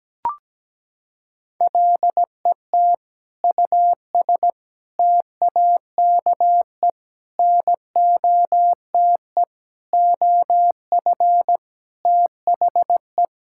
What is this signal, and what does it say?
Morse code, heard as a keyed tone, about 17 wpm, 700 Hz. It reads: LET US TAKE NOTE OF THE
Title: Sentences from Top 1600 Words 17wpm